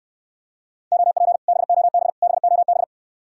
Morse code, heard as a keyed tone, about 69 wpm, 700 Hz. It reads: OP BOB BOB